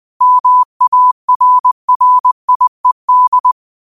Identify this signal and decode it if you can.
Morse code, heard as a keyed tone, about 20 wpm, 1000 Hz. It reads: MARRIED
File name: qotdfiles / 20wpm